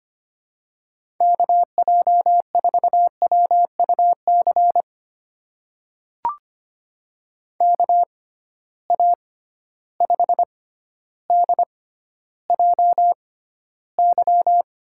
Morse code, heard as a keyed tone, about 25 wpm, spaced as for 13 wpm, 700 Hz. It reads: KJ4WUC KA5DJY
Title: USA Call Signs - 6 Characters - 6x Character Spacing - 25wpm